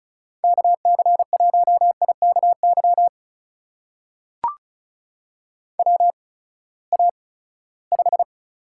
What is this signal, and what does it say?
Morse code, heard as a keyed tone, about 36 wpm, spaced as for 20 wpm, 700 Hz. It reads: KC1IKY WA5
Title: USA Call Signs - 6 Characters - 8x Character Spacing - 35wpm